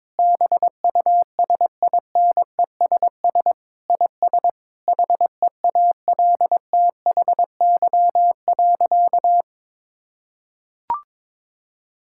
Morse code, heard as a keyed tone, about 22 wpm, 700 Hz. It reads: BUSINESS IS HEALTHY.
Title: Sentences from Top 1000 Words - Review Part 5 - 22wpm